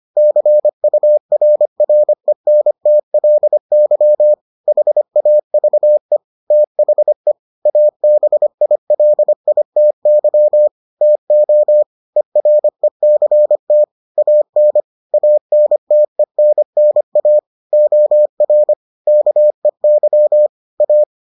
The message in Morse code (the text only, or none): CURRENTLY HAVE THE ABILITY TO ERECT AN ANTENNA OR KEY A